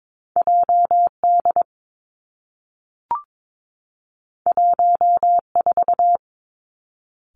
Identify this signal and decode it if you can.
Morse code, heard as a keyed tone, about 22 wpm, 700 Hz. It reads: J D 1 4
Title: Two Challenging Letter-Number Combinations 22wpm